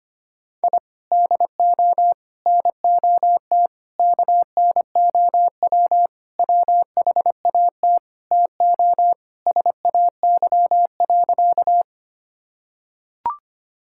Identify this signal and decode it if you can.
Morse code, heard as a keyed tone, about 25 wpm, 700 Hz. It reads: I DO NOT KNOW WHAT TO SAY.
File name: practice / 25wpm / Sentences with 4-Word Phrases - 25wpm